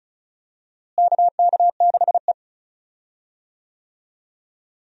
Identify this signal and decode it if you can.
Morse code, heard as a keyed tone, about 34 wpm, 700 Hz. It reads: KK6E
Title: USA Call Signs - 4 Characters - Repeated 3x - 35wpm